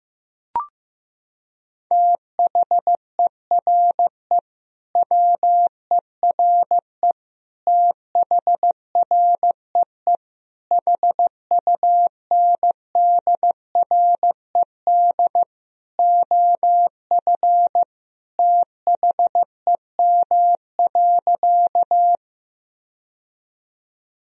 Morse code, heard as a keyed tone, about 15 wpm, 700 Hz. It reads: THERE WERE THREE HUNDRED OF THEM.
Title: Sentences from Top 1000 Words - Review Part 2 - 15wpm